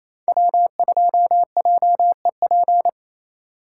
Morse code, heard as a keyed tone, about 32 wpm, 700 Hz. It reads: W2JEP
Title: Speed Racing - USA 1x3 Call Signs 28wpm